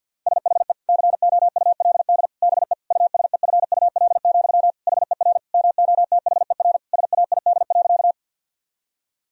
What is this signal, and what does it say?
Morse code, heard as a keyed tone, about 50 wpm, 700 Hz. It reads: SHE COULD BE USEFUL, HER MOTHER SAID.
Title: Sentences from Top 1000 Words - Review Part 4 - 50wpm